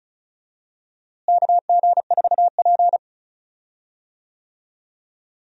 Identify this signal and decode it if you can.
Morse code, heard as a keyed tone, about 36 wpm, 700 Hz. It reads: KG4P
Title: USA Call Signs - 4 Characters - Repeated 3x - 35wpm